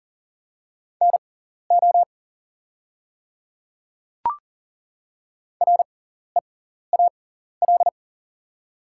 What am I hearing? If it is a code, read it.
Morse code, heard as a keyed tone, about 40 wpm, spaced as for 14 wpm, 700 Hz. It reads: NO REAL
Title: 6x Character Spacing - Top 300 Words - 40wpm